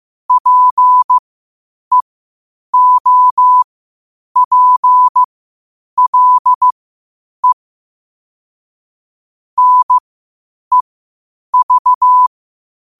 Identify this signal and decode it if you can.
Morse code, heard as a keyed tone, about 15 wpm, spaced as for 8 wpm, 1000 Hz. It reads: PEOPLE NEV
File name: qotdfiles / 8wpm